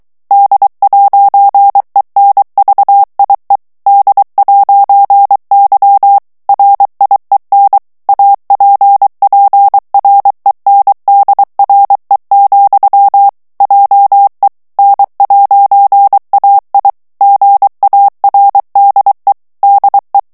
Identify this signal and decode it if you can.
Morse code, heard as a keyed tone, about 23 wpm, 800 Hz. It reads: D'ENVIE D'Y RIEN APPRENDRE, JE N'AI GARDE DE
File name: FRANCAIS TEXTE CLAIR 2 VITESSE 22 MOTS MINUTES